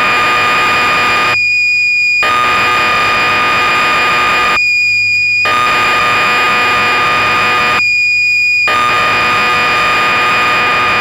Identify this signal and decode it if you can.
Cassette data (Kansas City type).
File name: magazines / Acorn User / 1984-10 BBC Programs